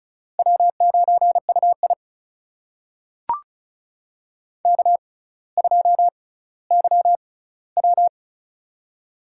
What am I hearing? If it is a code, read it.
Morse code, heard as a keyed tone, about 36 wpm, spaced as for 18 wpm, 700 Hz. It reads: W9UI K2YW
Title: USA Call Signs - 4 Characters - 6x Character Spacing - 35wpm